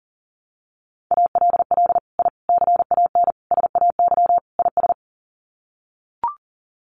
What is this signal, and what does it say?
Morse code, heard as a keyed tone, about 40 wpm, 700 Hz. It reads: ALL I CAN SAY IS
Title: Sets of 5 Words Encore2 40wpm